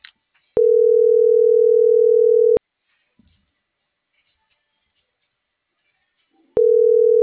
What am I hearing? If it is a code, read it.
Ringback tone (North America), 440 + 480 Hz.